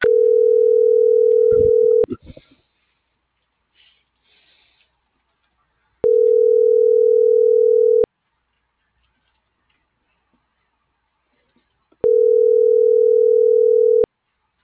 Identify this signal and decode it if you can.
Ringback tone (North America), 440 + 480 Hz.